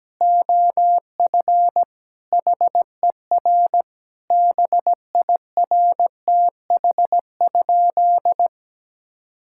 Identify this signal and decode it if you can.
Morse code, heard as a keyed tone, about 17 wpm, 700 Hz. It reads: OF HER BIRTH?
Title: Sentences from Top 1000 Words - Review Part 5 - 17wpm